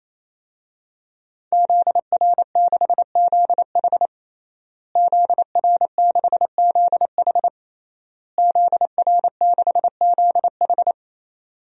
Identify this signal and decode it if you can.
Morse code, heard as a keyed tone, about 27 wpm, 700 Hz. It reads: ZR6ZH ZR6ZH ZR6ZH
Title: DXCC-100 Call Signs - 5 Characters - Repeated 3x - 28wpm